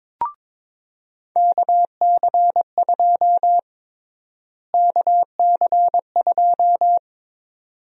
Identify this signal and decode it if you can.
Morse code, heard as a keyed tone, about 22 wpm, 700 Hz. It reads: KC2 KC2
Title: USA Call Sign Prefixes - Repeated 2x - 22wpm